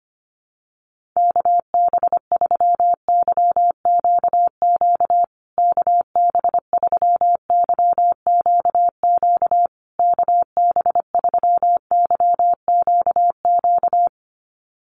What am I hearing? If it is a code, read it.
Morse code, heard as a keyed tone, about 25 wpm, 700 Hz. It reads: KB3YQQ KB3YQQ KB3YQQ
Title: USA Call Signs Repeated 3x 25wpm